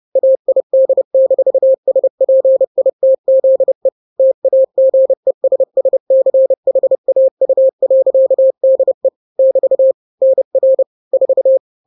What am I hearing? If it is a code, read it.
Morse code, heard as a keyed tone, about 30 wpm, 530 Hz. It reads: AID-SPITZE TAGESSCHAU.DE <BT> NR 4
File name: 29wpm